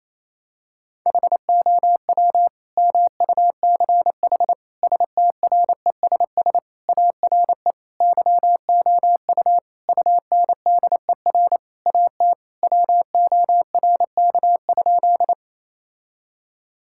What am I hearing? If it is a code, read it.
Morse code, heard as a keyed tone, about 28 wpm, 700 Hz. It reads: HOW MUCH STRESS ARE YOU UNDER AT WORK?
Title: Sentences from Top 1000 Words - Review Part 6 - 28wpm